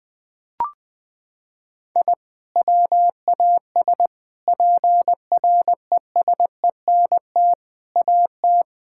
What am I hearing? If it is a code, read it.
Morse code, heard as a keyed tone, about 20 wpm, 700 Hz. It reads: I WAS PRESENT AT
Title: Sentences from Top 1000 Words - Review Part 5 - 20wpm